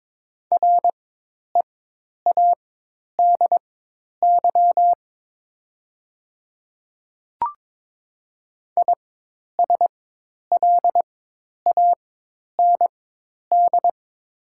Morse code, heard as a keyed tone, about 22 wpm, spaced as for 10 wpm, 700 Hz. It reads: READY ISLAND
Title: 4x Character Spacing - Top 500 Words - 22wpm